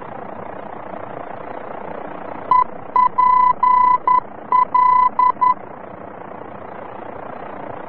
Morse code, heard as a keyed tone, about 11 wpm, 1000 Hz. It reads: EPL
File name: france / mx / EPL-VOR-F